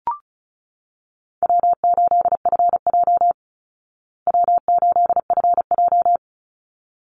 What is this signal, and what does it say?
Morse code, heard as a keyed tone, about 34 wpm, 700 Hz. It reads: W8FJ W8FJ
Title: Easy Call Signs Repeated 2x - 35wpm